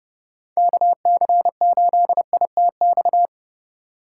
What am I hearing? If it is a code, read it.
Morse code, heard as a keyed tone, about 30 wpm, 700 Hz. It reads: KC8ITX
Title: USA Call Signs - 6 Characters - 3x Character Spacing - 30wpm